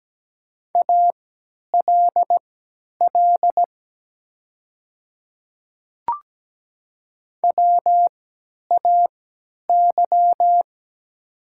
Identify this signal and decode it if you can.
Morse code, heard as a keyed tone, about 17 wpm, spaced as for 10 wpm, 700 Hz. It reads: ALL WAY
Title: 3x Character Spacing - Top 100 Words - 17wpm